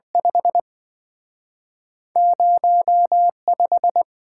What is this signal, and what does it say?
Morse code, heard as a keyed tone, about 27 wpm, 700 Hz. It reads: 5 05